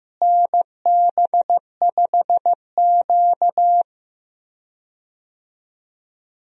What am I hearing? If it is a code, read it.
Morse code, heard as a keyed tone, about 15 wpm, 700 Hz. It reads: NB5Q
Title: USA Call Signs - 4 Characters - Repeated 3x - 15wpm